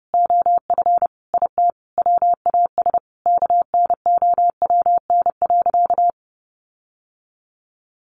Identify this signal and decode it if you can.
Morse code, heard as a keyed tone, about 30 wpm, 700 Hz. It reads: OF IT WAS KNOWN.
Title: Sentences from Top 1400 Words 30wpm